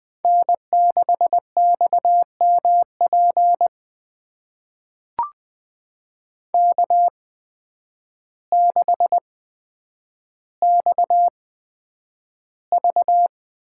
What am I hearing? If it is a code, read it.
Morse code, heard as a keyed tone, about 20 wpm, spaced as for 9 wpm, 700 Hz. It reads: N6XMP K6XV
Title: USA Call Signs - 8x Character Spacing - 20wpm